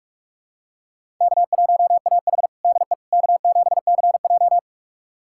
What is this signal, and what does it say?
Morse code, heard as a keyed tone, about 45 wpm, 700 Hz. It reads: K1AH DE K7CJ
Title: USA Callsign DE Callsign - 4 Characters - 45wpm